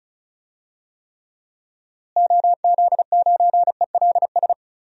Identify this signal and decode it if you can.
Morse code, heard as a keyed tone, about 36 wpm, 700 Hz. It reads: OZ9ELS
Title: DXCC-100 Call Signs - 6 Characters - Repeated 2x - 35wpm